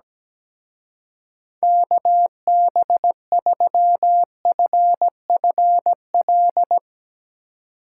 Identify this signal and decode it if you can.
Morse code, heard as a keyed tone, about 18 wpm, 700 Hz. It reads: KB3FFL
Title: Speed Racing - USA 2x3 Call Signs 17wpm